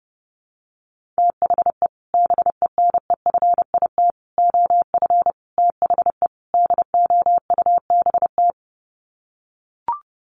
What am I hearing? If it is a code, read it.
Morse code, heard as a keyed tone, about 30 wpm, 700 Hz. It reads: THE BENEFIT OF THE DOUBT
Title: Sets of 5 Words Encore 30wpm